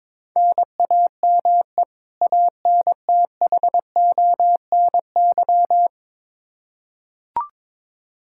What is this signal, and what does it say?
Morse code, heard as a keyed tone, about 22 wpm, 700 Hz. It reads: NAME ANTHONY
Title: K1USN SST QSO Practice 22wpm